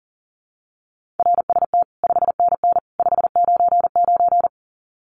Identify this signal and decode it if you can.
Morse code, heard as a keyed tone, about 40 wpm, 700 Hz. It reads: RST 5NN 599
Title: Common QSO Phrases 40wpm